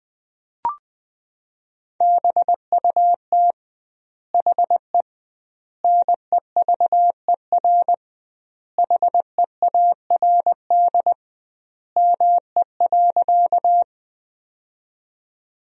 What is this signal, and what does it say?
Morse code, heard as a keyed tone, about 20 wpm, 700 Hz. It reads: BUT HE NEVER HEARD ME.
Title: 2x Word-Spacing - Sentences from Top 400 Words - 20wpm